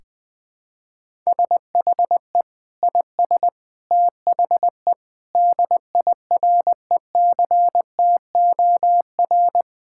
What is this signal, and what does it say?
Morse code, heard as a keyed tone, about 21 wpm, 700 Hz. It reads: SHE IS THE DIRECTOR
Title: Sentences with 3-Word Phrases - 20wpm